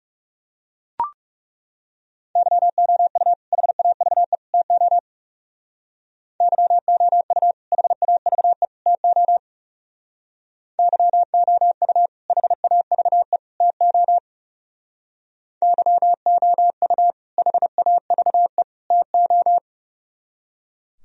Morse code, heard as a keyed tone, about 38 wpm, 700 Hz. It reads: YOU HAVE TO YOU HAVE TO YOU HAVE TO YOU HAVE TO